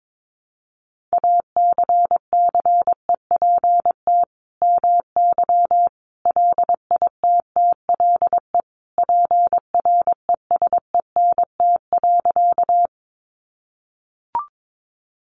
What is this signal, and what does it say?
Morse code, heard as a keyed tone, about 22 wpm, 700 Hz. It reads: ACCEPT MY LITTLE PRESENT.
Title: Sentences from Top 1300 Words 22wpm